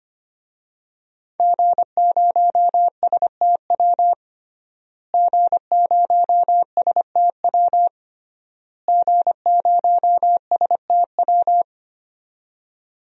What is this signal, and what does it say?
Morse code, heard as a keyed tone, about 25 wpm, 700 Hz. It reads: G0STW G0STW G0STW
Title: DXCC-100 Call Signs - 5 Characters - Repeated 3x - 25wpm